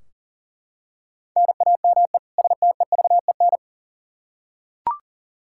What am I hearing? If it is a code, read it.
Morse code, heard as a keyed tone, about 40 wpm, 700 Hz. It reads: NAME STEVEN
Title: Practice QSO - Exchange 1 - Beginning - 40wpm